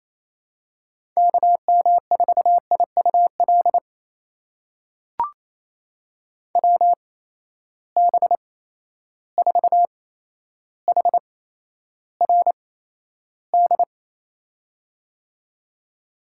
Morse code, heard as a keyed tone, about 27 wpm, spaced as for 12 wpm, 700 Hz. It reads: KM4IUL WB4HRD
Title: USA Call Signs - 6 Characters - 8x Character Spacing - 28wpm